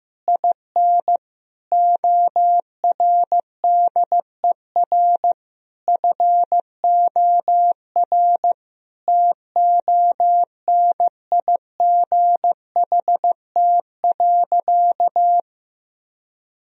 Morse code, heard as a keyed tone, about 15 wpm, 700 Hz. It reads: IN ORDER FOR TONIGHT.
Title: Sentences from Top 1000 Words - Review Part 3 - 15wpm